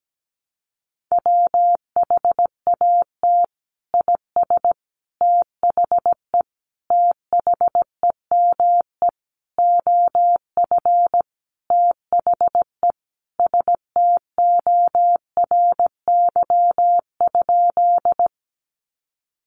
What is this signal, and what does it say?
Morse code, heard as a keyed tone, about 17 wpm, 700 Hz. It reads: WHAT IS THE THEME OF THE STORY?
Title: Sentences from Top 1100 Words 17wpm